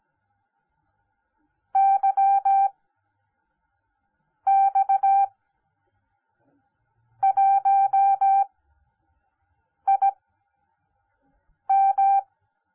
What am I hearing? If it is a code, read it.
Morse code, heard as a keyed tone, about 17 wpm, spaced as for 5 wpm, 780 Hz. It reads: YX1IM